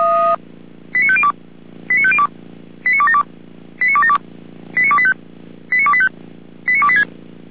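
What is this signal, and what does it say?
Five-tone selective call (ZVEI), heard as quick runs of five tones, tones of about 70 ms. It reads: 89502 89502 89272 89272 89285 89285 89286
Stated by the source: Five-tone ZVEI selective call received by SDR, from the audio archive of the Feuerwehr Orlamünde fire brigade.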